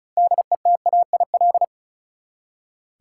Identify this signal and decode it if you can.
Morse code, heard as a keyed tone, about 34 wpm, 700 Hz. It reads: DETAIL
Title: Six Letter Words - 35wpm